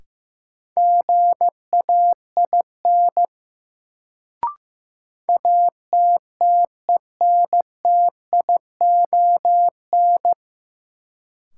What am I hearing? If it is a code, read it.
Morse code, heard as a keyed tone, about 15 wpm, 700 Hz. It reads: GAIN ATTENTION
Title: Sentences from Top 1000 Words - New Word Drill - 15wpm